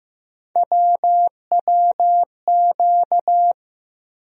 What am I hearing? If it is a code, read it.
Morse code, heard as a keyed tone, about 15 wpm, 700 Hz. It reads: WWQ
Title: USA Call Sign Postfixes - 3 Letters 15wpm